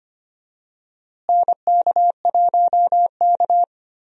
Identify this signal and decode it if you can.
Morse code, heard as a keyed tone, about 25 wpm, 700 Hz. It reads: NK1K
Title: USA Call Signs - 4 Characters - Repeated 2x - 25wpm